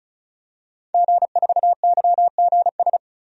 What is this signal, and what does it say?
Morse code, heard as a keyed tone, about 36 wpm, 700 Hz. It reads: G4YGS
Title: DXCC-100 Call Signs - 5 Characters - 35wpm